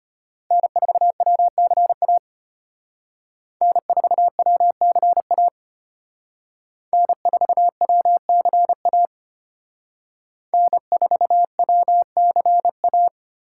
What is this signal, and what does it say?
Morse code, heard as a keyed tone, about 32 wpm, 700 Hz. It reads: N4WCA N4WCA N4WCA N4WCA